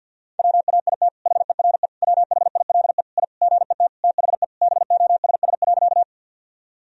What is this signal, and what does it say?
Morse code, heard as a keyed tone, about 47 wpm, 700 Hz. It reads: WAIT HERE WHILE I GET THE BOSS.
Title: Sentences from Top 1000 Words - Review Part 4 - 50wpm